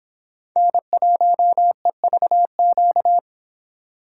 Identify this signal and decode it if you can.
Morse code, heard as a keyed tone, about 26 wpm, 700 Hz. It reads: N1EVQ